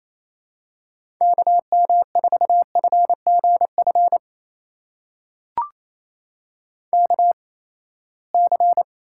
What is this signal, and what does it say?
Morse code, heard as a keyed tone, about 28 wpm, spaced as for 19 wpm, 700 Hz. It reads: KM4FGF KC
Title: USA Call Signs - 6 Characters - 8x Character Spacing - 28wpm